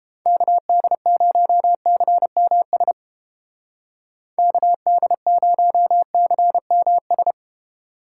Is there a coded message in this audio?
Yes: Morse code from 0.3 to 7.3 s, heard as a keyed tone, about 33 wpm, 700 Hz. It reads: KD0CMS KD0CMS